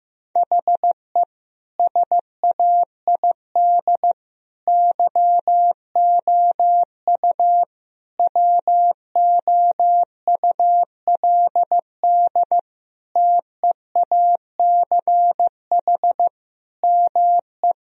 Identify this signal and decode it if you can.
Morse code, heard as a keyed tone, about 15 wpm, 700 Hz. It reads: HE SAID YOU WOULD TEACH ME
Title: Sentences from Top 1000 Words - Review Part 2 - 15wpm